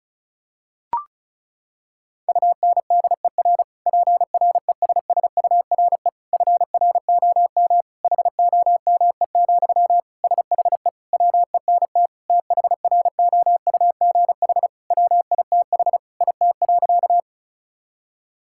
Morse code, heard as a keyed tone, about 36 wpm, 700 Hz. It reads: UNDER PRESSURE FROM HOME, SHE WENT THROUGH WITH IT.
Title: Sentences from Top 1000 Words - Review Part 6 - 35wpm